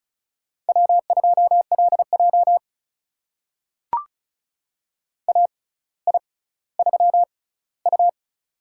Morse code, heard as a keyed tone, about 36 wpm, spaced as for 18 wpm, 700 Hz. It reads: W2LJ AI3U
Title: USA Call Signs - 4 Characters - 6x Character Spacing - 35wpm